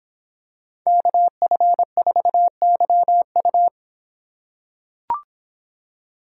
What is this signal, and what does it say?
Morse code, heard as a keyed tone, about 25 wpm, 700 Hz. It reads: KF4YU